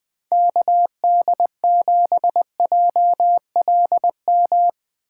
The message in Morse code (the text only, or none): KD7JLM